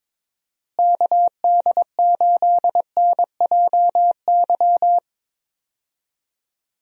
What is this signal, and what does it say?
Morse code, heard as a keyed tone, about 22 wpm, 700 Hz. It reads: KD8NJY
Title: USA Call Signs - 6 Characters - Repeated 2x - 22wpm